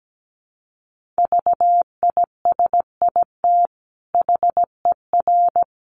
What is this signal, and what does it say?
Morse code, heard as a keyed tone, about 17 wpm, 700 Hz. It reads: VISIT HER
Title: Sentences from Top 900 Words 17wpm